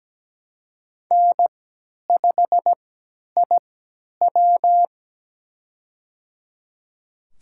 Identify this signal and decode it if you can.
Morse code, heard as a keyed tone, about 17 wpm, spaced as for 10 wpm, 700 Hz. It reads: N5IW
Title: USA Call Signs - 3x Character Spacing - 17wpm